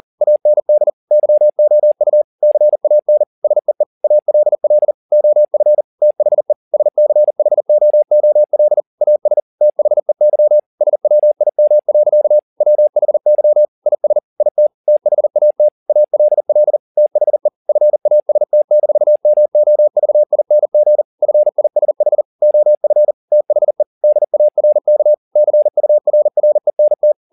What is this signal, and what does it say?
Morse code, heard as a keyed tone, about 40 wpm, 600 Hz. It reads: AND YOU CAN SEE ALL OF THE SCHOOL AS THEY SWIM. WHY IS IT THAT ALL THE FAST-MOVING FISH OF THE DARK CURRENT
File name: the old man and the sea 2 40wpm